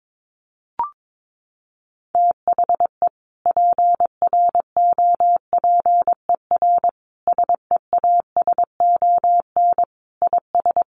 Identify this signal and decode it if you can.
Morse code, heard as a keyed tone, about 22 wpm, 700 Hz. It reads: THE PROPER SEASON IS
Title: Sentences from Top 1400 Words 22wpm